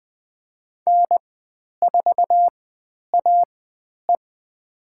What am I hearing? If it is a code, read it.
Morse code, heard as a keyed tone, about 20 wpm, spaced as for 10 wpm, 700 Hz. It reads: N4AE
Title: 109 Lesson 18g - Letter m - Call Signs 20wpm